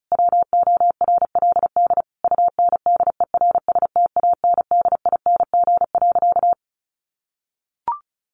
Morse code, heard as a keyed tone, about 34 wpm, 700 Hz. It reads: WORLD UNDERSTANDING.
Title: Sentences from Top 1400 Words 35wpm